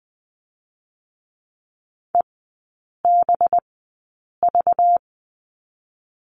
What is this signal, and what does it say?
Morse code, heard as a keyed tone, about 19 wpm, spaced as for 8 wpm, 700 Hz. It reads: EBV
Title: 205 Lesson 33e - Letter j - Three Characters 20wpm